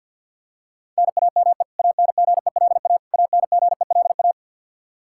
Morse code, heard as a keyed tone, about 49 wpm, 700 Hz. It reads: NAME ANGELA ANGELA
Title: Practice QSO - Exchange 1 - Beginning - 50wpm